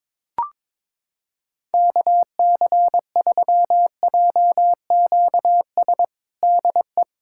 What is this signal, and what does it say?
Morse code, heard as a keyed tone, about 22 wpm, 700 Hz. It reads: KC3JQS DE
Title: USA Callsign DE Callsign 22wpm